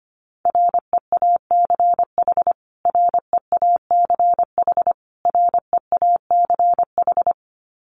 Morse code, heard as a keyed tone, about 25 wpm, 700 Hz. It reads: REACH REACH REACH
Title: Top 500 Words Repeated 25wpm